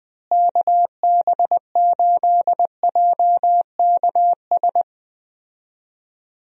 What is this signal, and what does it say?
Morse code, heard as a keyed tone, about 20 wpm, 700 Hz. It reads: KB8JKS
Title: USA Call Signs - 6 Characters - Repeated 2x - 20wpm